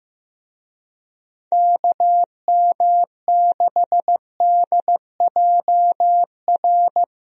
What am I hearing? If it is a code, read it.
Morse code, heard as a keyed tone, about 15 wpm, 700 Hz. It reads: KM6DJR
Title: USA Call Signs - 6 Characters - Repeated 3x - 15wpm